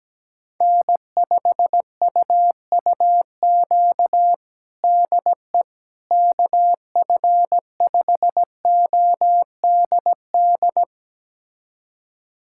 Morse code, heard as a keyed tone, about 17 wpm, 700 Hz. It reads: N5UUQ DE KF5ODD
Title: USA Callsign DE Callsign 17wpm